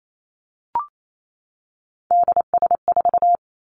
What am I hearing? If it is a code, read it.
Morse code, heard as a keyed tone, about 28 wpm, 700 Hz. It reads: DS4
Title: Three Challenging Letter-Number Combinations 28wpm